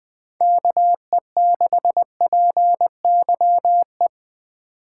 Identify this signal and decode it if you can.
Morse code, heard as a keyed tone, about 21 wpm, 700 Hz. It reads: KE6PYE
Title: USA Call Signs - 6 Characters - 8x Character Spacing - 20wpm